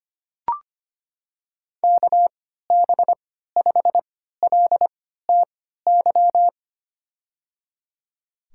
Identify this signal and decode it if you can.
Morse code, heard as a keyed tone, about 25 wpm, spaced as for 14 wpm, 700 Hz. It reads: KB5LTY
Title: USA Call Signs - 3x Character Spacing - 25wpm